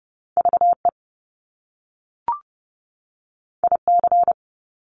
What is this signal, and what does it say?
Morse code, heard as a keyed tone, about 30 wpm, 700 Hz. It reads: VE IC
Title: Top Two Letter Combinations in Words 30wpm